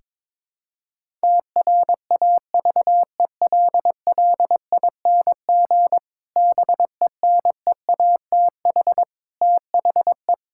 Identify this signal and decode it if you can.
Morse code, heard as a keyed tone, about 22 wpm, 700 Hz. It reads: TRAVELLING BENEATH THE
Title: Pangram Challenge v2 - 22wpm